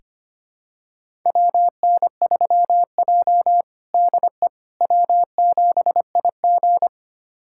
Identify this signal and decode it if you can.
Morse code, heard as a keyed tone, about 25 wpm, 700 Hz. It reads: WN3J DE W7IG
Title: USA Callsign DE Callsign - 4 Characters - 25wpm